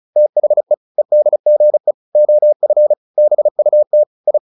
Morse code, heard as a keyed tone, about 34 wpm, 600 Hz. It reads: THE EDGE OF BUT I